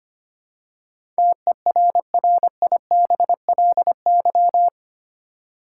Morse code, heard as a keyed tone, about 25 wpm, 700 Hz. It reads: TERRIBLY
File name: 8 Letter Words - First 3 Letters Given - 25wpm